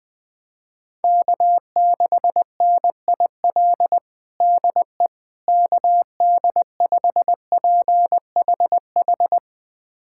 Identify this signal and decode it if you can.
Morse code, heard as a keyed tone, about 21 wpm, 700 Hz. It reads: K6NIL DE KD5PHH
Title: USA Callsign DE Callsign 20wpm